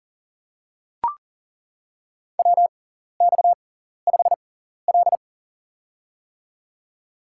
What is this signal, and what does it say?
Morse code, heard as a keyed tone, about 40 wpm, spaced as for 14 wpm, 700 Hz. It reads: WX5L
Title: USA Call Signs - 4 Characters - 6x Character Spacing - 40wpm